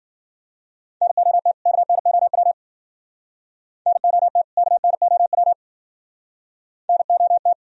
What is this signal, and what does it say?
Morse code, heard as a keyed tone, about 52 wpm, 700 Hz. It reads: NOT KNOW NOT KNOW NOT